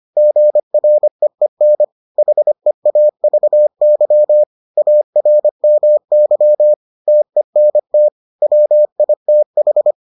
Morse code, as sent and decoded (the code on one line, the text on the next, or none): --. .-. . -. / .... .- ...- -.-- / .- .-. -- -.-- / - . -. - / .-- .. - ....
GREEN HEAVY ARMY TENT WITH